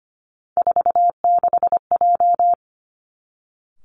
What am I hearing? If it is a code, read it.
Morse code, heard as a keyed tone, about 25 wpm, 700 Hz. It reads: 46J
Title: Three Challenging Letter-Number Combinations 25wpm